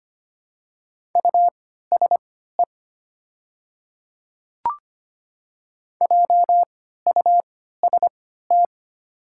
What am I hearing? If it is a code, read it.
Morse code, heard as a keyed tone, about 25 wpm, spaced as for 14 wpm, 700 Hz. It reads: USE JUST
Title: 3x Character Spacing - Top 200 Words - 25wpm